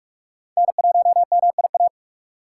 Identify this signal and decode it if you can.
Morse code, heard as a keyed tone, about 43 wpm, 700 Hz. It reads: N1MIA
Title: USA Call Signs - 3x Character Spacing - 45wpm